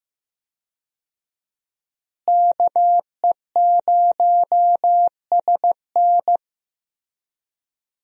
Morse code, heard as a keyed tone, about 15 wpm, 700 Hz. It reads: KE0SN
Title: USA Call Signs - 5 Characters - Repeated 3x - 15wpm